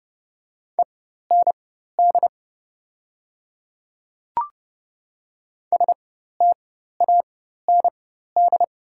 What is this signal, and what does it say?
Morse code, heard as a keyed tone, about 30 wpm, spaced as for 14 wpm, 700 Hz. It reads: END STAND
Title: 4x Character Spacing - Top 300 Words - 30wpm